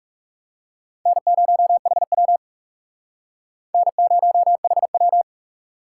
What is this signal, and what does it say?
Morse code, heard as a keyed tone, about 36 wpm, 700 Hz. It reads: N0HW N0HW
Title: Speed Racing - USA 1x2 Call Signs 30wpm